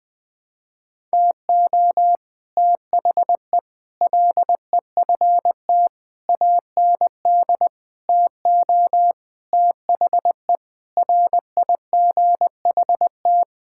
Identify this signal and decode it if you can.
Morse code, heard as a keyed tone, about 20 wpm, 700 Hz. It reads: TO THE LEFT AND TO THE RIGHT
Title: Sets of 9 Words - 20wpm